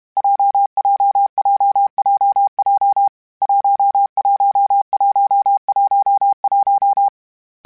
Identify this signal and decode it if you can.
Morse code, heard as a keyed tone, about 32 wpm, 800 Hz. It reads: JJJJJ 11111
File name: Difficult Combos / Difficult Combos-mp3s / WPJ WPJ1 @30